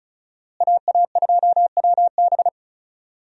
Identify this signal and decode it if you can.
Morse code, heard as a keyed tone, about 36 wpm, 700 Hz. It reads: AA2WB
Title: USA Call Signs - 5 Characters - Repeated 3x - 35wpm